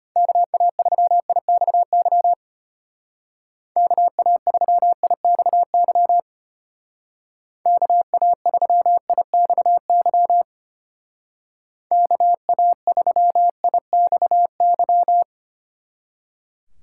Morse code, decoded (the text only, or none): KA3IXY KA3IXY KA3IXY KA3IXY